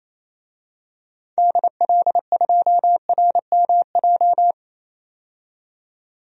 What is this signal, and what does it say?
Morse code, heard as a keyed tone, about 28 wpm, 700 Hz. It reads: DL2RMJ
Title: DXCC-100 Call Signs - 6 Characters - 28wpm